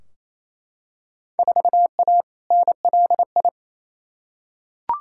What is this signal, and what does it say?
Morse code, heard as a keyed tone, about 28 wpm, 700 Hz. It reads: 4A NLI
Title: Field Day Exchanges Encore - 28wpm